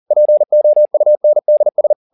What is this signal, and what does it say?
Morse code, heard as a keyed tone, about 40 wpm, 600 Hz. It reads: POUNDS
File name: the old man and the sea 2 40wpm